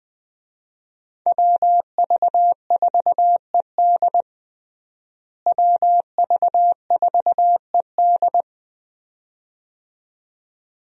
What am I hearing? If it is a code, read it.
Morse code, heard as a keyed tone, about 20 wpm, 700 Hz. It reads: WV4ED WV4ED
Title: USA Call Signs - 5 Characters - Repeated 2x - 20wpm